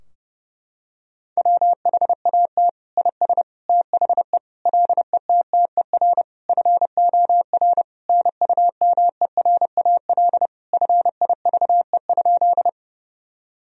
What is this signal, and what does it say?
Morse code, heard as a keyed tone, about 32 wpm, 700 Hz. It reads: WHAT IS THE LETTER FOR NUMERAL FIVE?
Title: Sentences from Top 1000 Words - Review Part 2 - 30wpm